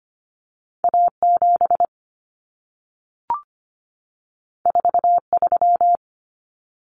Morse code, heard as a keyed tone, about 25 wpm, 700 Hz. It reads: A7 43
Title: Common Cut Numbers - 2-Digit 25wpm